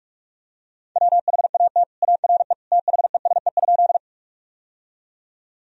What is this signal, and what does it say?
Morse code, heard as a keyed tone, about 46 wpm, 700 Hz. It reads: WHAT ARE THESE?
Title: Sentences from Top 1000 Words - Review Part 1 - 45wpm